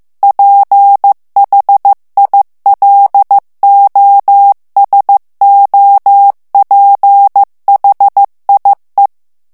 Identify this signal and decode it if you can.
Morse code, heard as a keyed tone, about 15 wpm, 800 Hz. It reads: PHILOSOPHIE
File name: FRANCAIS TEXTE CLAIR 2 VITESSE 14 MOTS MINUTES